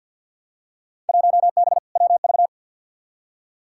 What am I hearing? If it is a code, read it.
Morse code, heard as a keyed tone, about 47 wpm, 700 Hz. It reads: 1B WV
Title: Field Day Exchanges Encore - 50wpm